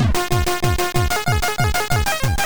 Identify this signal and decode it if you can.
There is no code.